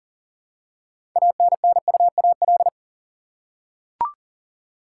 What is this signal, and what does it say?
Morse code, heard as a keyed tone, about 40 wpm, 700 Hz. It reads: ANNUAL
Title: Six Letter Words - 40wpm